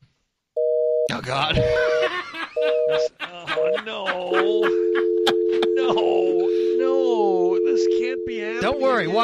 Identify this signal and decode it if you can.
Telephone call progress tone (North America), heard as a telephone network's tone, about 480, 620 Hz, with speech. Busy tone.